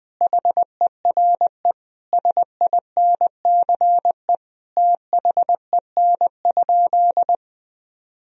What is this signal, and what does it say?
Morse code, heard as a keyed tone, about 21 wpm, 700 Hz. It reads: HERE SINCE THEN?
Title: Sentences from Top 1000 Words - Review Part 1 - 20wpm